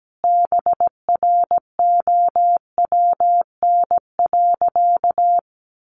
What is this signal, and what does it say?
Morse code, heard as a keyed tone, about 17 wpm, 700 Hz. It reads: BROWN.
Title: Sentences from Top 1200 Words 17wpm